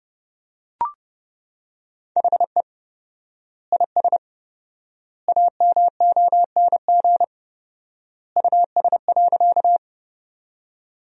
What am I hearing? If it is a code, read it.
Morse code, heard as a keyed tone, about 30 wpm, 700 Hz. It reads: HE IS AMONG US.
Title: 4x Word-Spacing - Sentences from Top 500 Words - 30wpm